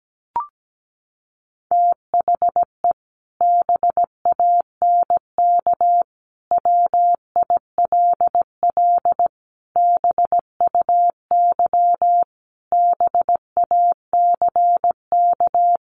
Morse code, heard as a keyed tone, about 17 wpm, 700 Hz. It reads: THE BANK WILL BUY BACK
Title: Sentences from Top 1500 Words 17wpm